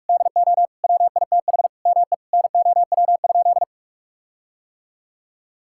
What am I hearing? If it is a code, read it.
Morse code, heard as a keyed tone, about 44 wpm, 700 Hz. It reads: DO WITH ME NOW?
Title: Sentences from Top 1600 Words 45wpm